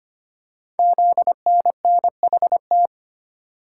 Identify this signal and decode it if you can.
Morse code, heard as a keyed tone, about 25 wpm, 700 Hz. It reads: ZNNHT
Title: Code Groups - 5 Letters - 25wpm